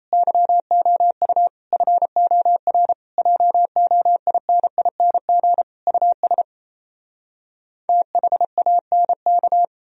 Morse code, heard as a keyed tone, about 38 wpm, 700 Hz. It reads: YOU FOR JOINING US THANK